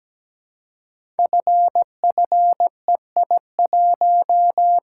Morse code, heard as a keyed tone, about 17 wpm, 700 Hz. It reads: FFEI1
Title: Code Groups - 5 Letter:Numbers - 17wpm